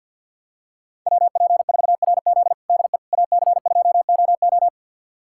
Morse code, heard as a keyed tone, about 51 wpm, 700 Hz. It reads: WP4RZ DE AC2OO
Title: USA Callsign DE Callsign - 5 Characters - 50wpm